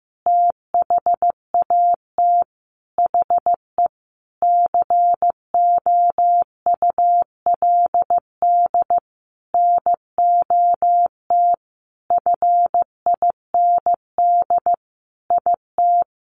Morse code, heard as a keyed tone, about 15 wpm, 700 Hz. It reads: THAT HE COULD NOT FIND IT
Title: Sentences from Top 1400 Words 15wpm